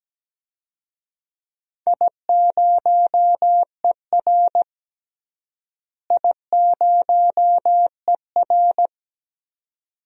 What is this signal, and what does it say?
Morse code, heard as a keyed tone, about 17 wpm, 700 Hz. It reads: I0ER I0ER
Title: DXCC-100 Call Signs - 4 Characters - Repeated 2x - 17wpm